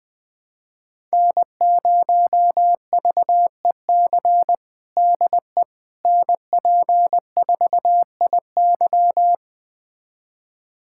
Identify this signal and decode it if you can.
Morse code, heard as a keyed tone, about 19 wpm, 700 Hz. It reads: N0VEC DE NP4IY
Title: USA Callsign DE Callsign - 5 Characters - 20wpm